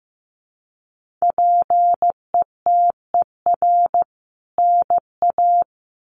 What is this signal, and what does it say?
Morse code, heard as a keyed tone, about 15 wpm, 700 Hz. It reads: PETER NA
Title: CWT Exchange 15wpm